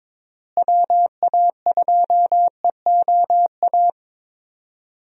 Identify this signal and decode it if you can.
Morse code, heard as a keyed tone, about 22 wpm, 700 Hz. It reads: WA2EOA
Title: USA Call Signs - 6 Characters - Repeated 3x - 22wpm